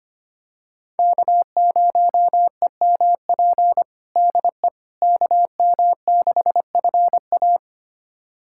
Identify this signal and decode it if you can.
Morse code, heard as a keyed tone, about 25 wpm, 700 Hz. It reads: K0EMP DE KM6FA
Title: USA Callsign DE Callsign - 5 Characters - 25wpm